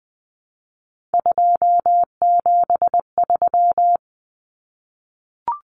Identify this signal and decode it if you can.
Morse code, heard as a keyed tone, about 20 wpm, 700 Hz. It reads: 273
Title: Numbers - 3-Digit 20wpm